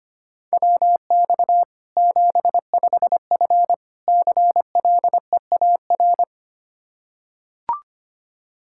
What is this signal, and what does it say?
Morse code, heard as a keyed tone, about 25 wpm, 700 Hz. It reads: WX 75F CLEAR
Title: Practice QSO - Exchange 2 - Info - 25wpm